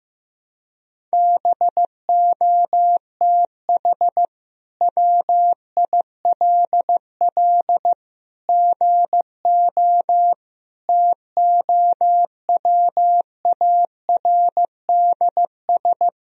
Morse code, heard as a keyed tone, about 15 wpm, 700 Hz. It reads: BOTH WILL GO TOWARDS
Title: Sentences from Top 1000 Words - Review Part 2 - 15wpm